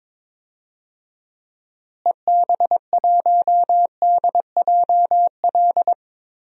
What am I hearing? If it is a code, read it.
Morse code, heard as a keyed tone, about 22 wpm, 700 Hz. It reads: EB1DJL
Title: DXCC-100 Call Signs - 6 Characters - Repeated 2x - 22wpm